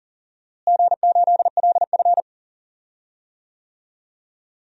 Morse code, heard as a keyed tone, about 37 wpm, 700 Hz. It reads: G8LF
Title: DXCC-100 Call Signs - 4 Characters - Repeated 3x - 40wpm